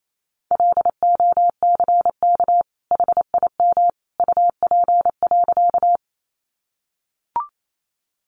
Morse code, heard as a keyed tone, about 28 wpm, 700 Hz. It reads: LOCK HIM UP.
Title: Sentences from Top 1300 Words 28wpm